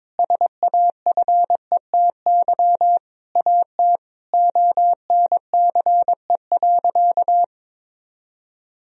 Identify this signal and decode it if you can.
Morse code, heard as a keyed tone, about 21 wpm, 700 Hz. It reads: SAFETY AT ONCE.
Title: Sentences from Top 1000 Words - Review Part 4 - 22wpm